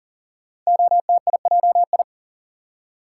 Morse code, heard as a keyed tone, about 38 wpm, 690 Hz. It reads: OTIJI
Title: Code Groups - 5 Letters - 40wpm